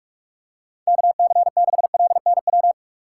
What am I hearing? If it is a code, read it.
Morse code, heard as a keyed tone, about 46 wpm, 700 Hz. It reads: KC6LNW
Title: USA Call Signs - 6 Characters - 3x Character Spacing - 45wpm